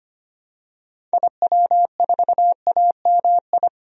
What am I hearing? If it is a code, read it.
Morse code, heard as a keyed tone, about 25 wpm, 700 Hz. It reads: IW4AMI
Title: DXCC-100 Call Signs - 6 Characters - Repeated 3x - 25wpm